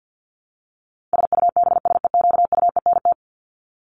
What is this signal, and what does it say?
Morse code, heard as a keyed tone, about 50 wpm, 700 Hz. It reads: SUBSEQUENT
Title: LY Suffix - Top 100 Words with Base Words 50wpm